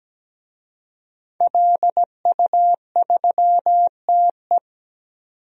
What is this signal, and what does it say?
Morse code, heard as a keyed tone, about 17 wpm, 700 Hz. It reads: LU3TE
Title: DXCC-100 Call Signs - 5 Characters - 17wpm